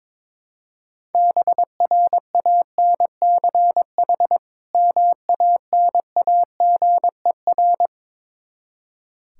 Morse code, heard as a keyed tone, about 22 wpm, 700 Hz. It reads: BRANCH MANAGER
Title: Practice QSO - Exchange 3 - More Info - 22wpm